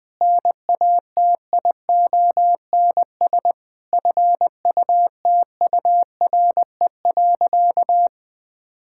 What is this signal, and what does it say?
Morse code, heard as a keyed tone, about 20 wpm, 700 Hz. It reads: NATIONS FUTURE.
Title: Sentences with 5-Word Phrases - 20wpm